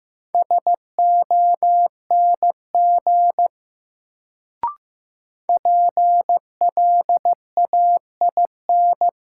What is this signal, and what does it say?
Morse code, heard as a keyed tone, about 15 wpm, 700 Hz. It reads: SONG PLAIN
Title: Sentences from Top 400 Words - New Word Drill - 15wpm